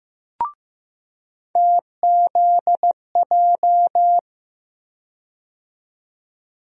Morse code, heard as a keyed tone, about 15 wpm, 700 Hz. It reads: TZJ
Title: USA Call Sign Postfixes - 3 Letters 15wpm